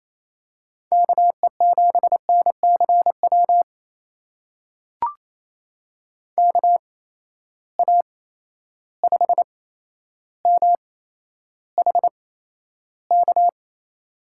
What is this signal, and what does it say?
Morse code, heard as a keyed tone, about 28 wpm, spaced as for 12 wpm, 700 Hz. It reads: KE7NCW KA5MHK
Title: USA Call Signs - 6 Characters - 8x Character Spacing - 28wpm